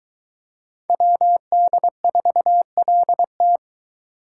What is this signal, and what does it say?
Morse code, heard as a keyed tone, about 23 wpm, 700 Hz. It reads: WD4LT